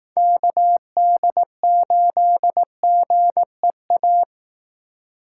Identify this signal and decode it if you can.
Morse code, heard as a keyed tone, about 18 wpm, 700 Hz. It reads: KD8GEA
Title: Speed Racing - USA 2x3 Call Signs 15wpm